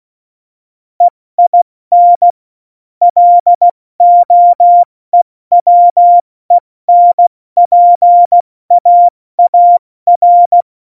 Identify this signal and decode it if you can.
Morse code, heard as a keyed tone, about 16 wpm, 700 Hz. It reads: EIN LOEWENPAAR
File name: Morsekurs / Witze / Witz 15. Mai 2018, 70 F80